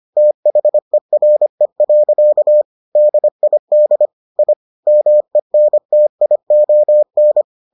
Morse code, heard as a keyed tone, about 25 wpm, 600 Hz. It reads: THERE. DID I MENTION